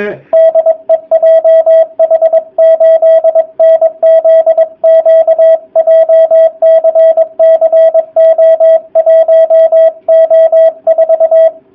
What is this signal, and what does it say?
Morse code, heard as a keyed tone, about 21 wpm, 660 Hz, with radio noise. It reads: DE JH8NZQ JCC O1O4
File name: jh8cbh / 2019cw / sp23- w